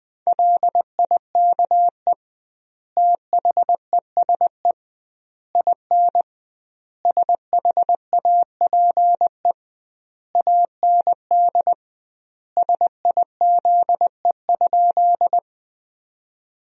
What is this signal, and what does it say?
Morse code, heard as a keyed tone, about 21 wpm, 700 Hz. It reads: LIKE THESE IN SHAPE AND SIZE?
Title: 2x Word-Spacing - Sentences from Top 500 Words - 20wpm